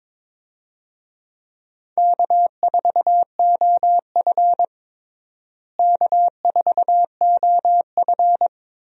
Morse code, heard as a keyed tone, about 22 wpm, 700 Hz. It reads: K4OF K4OF
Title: USA Call Signs - 4 Characters - Repeated 2x - 22wpm